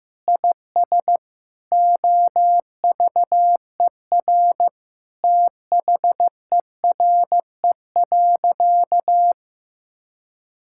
Morse code, heard as a keyed tone, about 15 wpm, 700 Hz. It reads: IS OVER THERE.
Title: Sentences from Top 1600 Words 15wpm